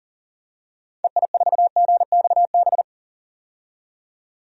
Morse code, heard as a keyed tone, about 40 wpm, 700 Hz. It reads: EI4GXB
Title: DXCC-100 Call Signs - 6 Characters - 40wpm